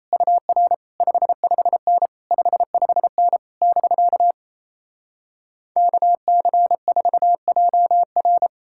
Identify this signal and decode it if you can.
Morse code, heard as a keyed tone, about 38 wpm, 700 Hz. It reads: UR 55N 55N <BK> KC4JR